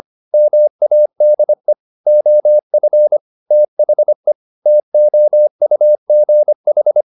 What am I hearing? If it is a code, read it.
Morse code, heard as a keyed tone, about 25 wpm, 600 Hz. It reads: MADE OF THE TOUGH
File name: the old man and the sea part 1 25wpm